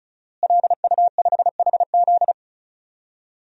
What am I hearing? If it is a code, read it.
Morse code, heard as a keyed tone, about 34 wpm, 700 Hz. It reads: LU5HZ
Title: DXCC-100 Call Signs - 5 Characters - Repeated 2x - 35wpm